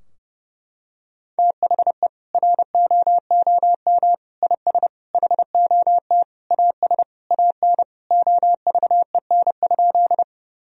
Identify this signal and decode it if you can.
Morse code, heard as a keyed tone, about 30 wpm, 700 Hz. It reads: THE ROOM IS HOT AS AN OVEN?
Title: Sentences from Top 1000 Words - Review Part 4 - 30wpm